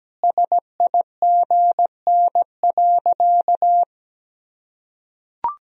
Morse code, heard as a keyed tone, about 17 wpm, 700 Hz. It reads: SIGN.
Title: Sentences from Top 1000 Words - Review Part 2 - 17wpm